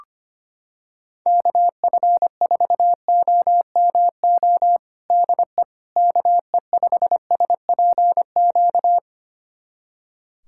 Morse code, heard as a keyed tone, about 25 wpm, 700 Hz. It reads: KF4OMO DE KE5SPQ
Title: USA Callsign DE Callsign - 6 Characters - 25wpm